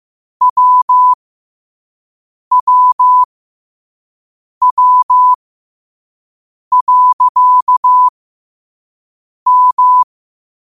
Morse code, heard as a keyed tone, about 15 wpm, spaced as for 5 wpm, 1000 Hz. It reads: WWW.M
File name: qotdfiles / 5wpm